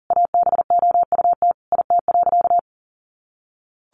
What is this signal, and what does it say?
Morse code, heard as a keyed tone, about 40 wpm, 700 Hz. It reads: ABOUT IT.
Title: Sentences from Top 200 Words 40wpm